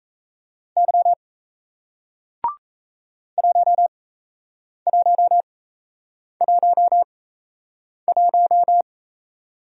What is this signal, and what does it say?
Morse code, heard as a keyed tone, about 38 wpm, spaced as for 8 wpm, 700 Hz. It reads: Y 1111